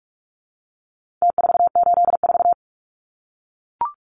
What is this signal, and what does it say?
Morse code, heard as a keyed tone, about 44 wpm, 700 Hz. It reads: T484
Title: Numbers - Contest Style - 4-Digit 45wpm